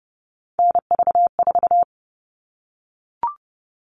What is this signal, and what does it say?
Morse code, heard as a keyed tone, about 30 wpm, 700 Hz. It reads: NV4
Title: All Cut Numbers - 3-Digit 30wpm